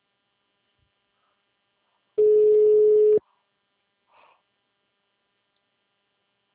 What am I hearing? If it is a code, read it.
Ringback tone (Europe, 425 Hz).